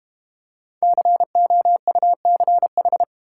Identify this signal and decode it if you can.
Morse code, heard as a keyed tone, about 32 wpm, 700 Hz. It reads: COUCH